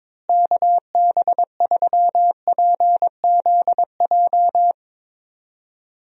Morse code, heard as a keyed tone, about 21 wpm, 700 Hz. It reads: KB3PZJ
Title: USA Call Signs - 6 Characters - Repeated 2x - 22wpm